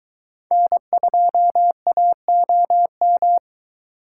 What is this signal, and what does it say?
Morse code, heard as a keyed tone, about 22 wpm, 700 Hz. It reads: N2AOM